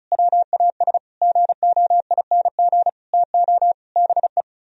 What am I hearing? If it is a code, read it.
Morse code, heard as a keyed tone, about 43 wpm, 700 Hz. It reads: WAS GOING TO BE
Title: Speed-Racing - Sets of 5 Words 35wpm